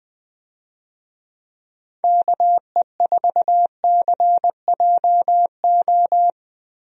Morse code, heard as a keyed tone, about 21 wpm, 700 Hz. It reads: KE4CJO
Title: USA Call Signs - 6 Characters - Repeated 2x - 20wpm